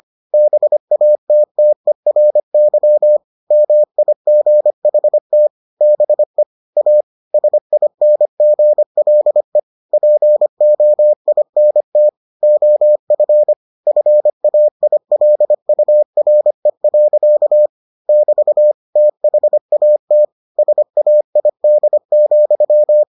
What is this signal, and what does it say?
Morse code, heard as a keyed tone, about 25 wpm, 600 Hz. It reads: BATTERY MIGHT BE A SINGLE POINT OF FAILURE. <BT> THAT SAID,